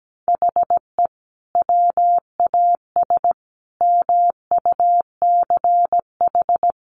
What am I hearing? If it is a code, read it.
Morse code, heard as a keyed tone, about 17 wpm, 700 Hz. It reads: HE WAS MUCH
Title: Sentences from Top 1200 Words 17wpm